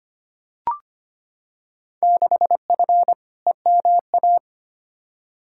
Morse code, heard as a keyed tone, about 25 wpm, 700 Hz. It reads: 6F EMA
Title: Field Day Exchanges Encore - 25wpm